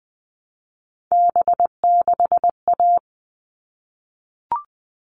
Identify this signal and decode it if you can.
Morse code, heard as a keyed tone, about 20 wpm, 700 Hz. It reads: B6A
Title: All Cut Numbers - 3-Digit 20wpm